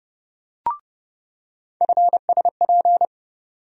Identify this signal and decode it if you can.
Morse code, heard as a keyed tone, about 32 wpm, 700 Hz. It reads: FSP
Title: USA Call Sign Postfixes - 3 Letters 30wpm